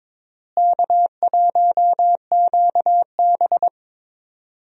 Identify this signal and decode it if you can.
Morse code, heard as a keyed tone, about 22 wpm, 700 Hz. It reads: K1QB
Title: USA Call Signs - 4 Characters - Repeated 2x - 22wpm